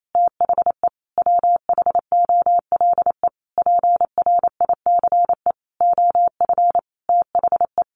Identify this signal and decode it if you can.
Morse code, heard as a keyed tone, about 28 wpm, 700 Hz. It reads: THE WHOLE PRICE OF THE
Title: Sentences from Top 1400 Words 28wpm